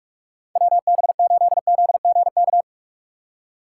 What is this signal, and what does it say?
Morse code, heard as a keyed tone, about 56 wpm, 700 Hz. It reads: WB8ZGK
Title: Speed Racing - USA 2x3 Call Signs 45wpm